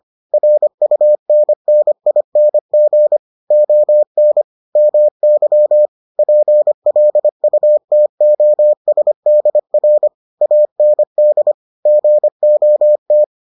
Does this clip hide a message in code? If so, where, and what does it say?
Yes: Morse code from 0.3 to 13.2 s, heard as a keyed tone, about 25 wpm, 600 Hz. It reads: RUNNING ON MY PLUTOSDR AND GOT